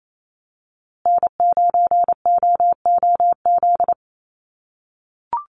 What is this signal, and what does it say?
Morse code, heard as a keyed tone, about 28 wpm, 700 Hz. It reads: N9OOZ
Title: USA Call Signs Repeated 3x 28wpm